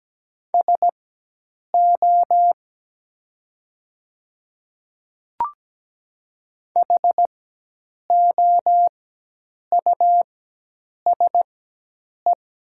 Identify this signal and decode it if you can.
Morse code, heard as a keyed tone, about 17 wpm, spaced as for 8 wpm, 700 Hz. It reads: SO HOUSE
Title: 4x Character Spacing - Top 200 Words - 17wpm